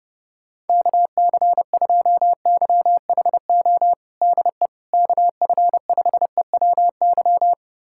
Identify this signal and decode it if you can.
Morse code, heard as a keyed tone, about 30 wpm, 700 Hz. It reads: KC2YHO DE KF5EWY
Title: USA Callsign DE Callsign - 6 Characters - 30wpm